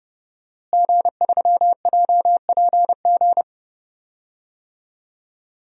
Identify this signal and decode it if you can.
Morse code, heard as a keyed tone, about 31 wpm, 700 Hz. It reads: G3JPG
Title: DXCC-100 Call Signs - 5 Characters - Repeated 2x - 30wpm